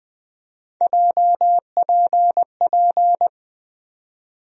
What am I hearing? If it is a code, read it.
Morse code, heard as a keyed tone, about 20 wpm, 700 Hz. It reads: JPP
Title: USA Call Sign Postfixes - 3 Letters 20wpm